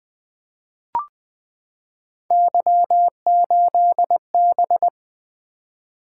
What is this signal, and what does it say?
Morse code, heard as a keyed tone, about 20 wpm, 700 Hz. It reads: Y8B
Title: Three Letter-Number Combinations - 20wpm